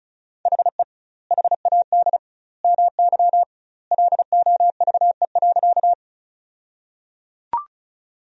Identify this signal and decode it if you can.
Morse code, heard as a keyed tone, about 34 wpm, 700 Hz. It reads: HE HAD MY LOVE.
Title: 2x Word-Spacing - Sentences from Top 400 Words - 35wpm